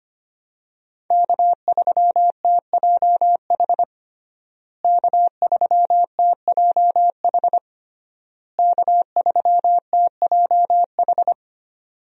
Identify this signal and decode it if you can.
Morse code, heard as a keyed tone, about 25 wpm, 700 Hz. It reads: K3TJH K3TJH K3TJH
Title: USA Call Signs - 5 Characters - Repeated 3x - 25wpm